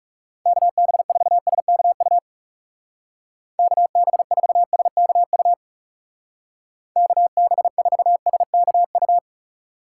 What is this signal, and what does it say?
Morse code, heard as a keyed tone, about 37 wpm, 700 Hz. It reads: KB4SKU KB4SKU KB4SKU
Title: Speed Racing - USA 2x3 Call Signs 30wpm